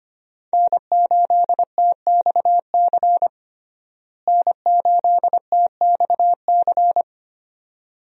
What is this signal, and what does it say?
Morse code, heard as a keyed tone, about 25 wpm, 700 Hz. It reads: N8TXC N8TXC
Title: USA Call Signs - 5 Characters - Repeated 2x - 25wpm